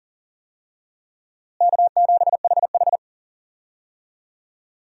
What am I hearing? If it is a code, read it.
Morse code, heard as a keyed tone, about 40 wpm, 700 Hz. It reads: K7HH
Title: USA Call Signs - 4 Characters - Repeated 2x - 40wpm